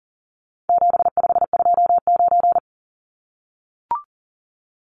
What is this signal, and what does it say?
Morse code, heard as a keyed tone, about 40 wpm, 700 Hz. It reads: 7529
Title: Numbers - 4-Digit 40wpm